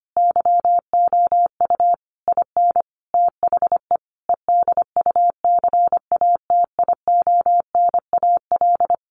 Morse code, heard as a keyed tone, about 25 wpm, 700 Hz. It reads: YOU IN THE EDUCATIONAL
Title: Sentences from Top 1200 Words 25wpm